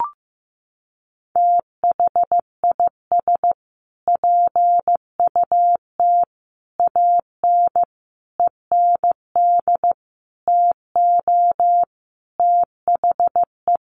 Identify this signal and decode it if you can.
Morse code, heard as a keyed tone, about 15 wpm, 700 Hz. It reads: THIS PUT AN END TO THE
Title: Sentences from Top 1100 Words 15wpm